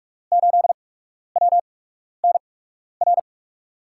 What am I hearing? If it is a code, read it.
Morse code, heard as a keyed tone, about 44 wpm, spaced as for 12 wpm, 700 Hz. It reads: 8WNR
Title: USA Call Signs - 8x Character Spacing - 45wpm